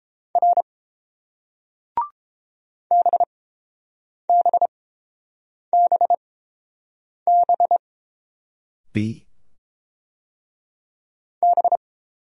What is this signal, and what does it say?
Morse code, heard as a keyed tone, about 28 wpm, spaced as for 7 wpm, 700 Hz. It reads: R BBBB B